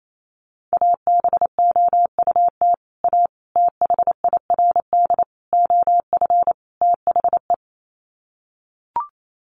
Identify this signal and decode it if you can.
Morse code, heard as a keyed tone, about 28 wpm, 700 Hz. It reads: ABOUT A THIRD OF THE
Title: Sets of 5 Words Encore2 28wpm